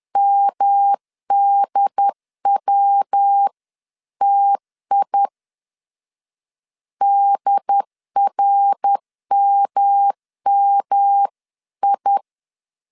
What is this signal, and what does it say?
Morse code, heard as a keyed tone, about 10 wpm, 800 Hz. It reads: MDWTI DRMMI
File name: CW / DET / 1 3 1500C S C 10wpm 3335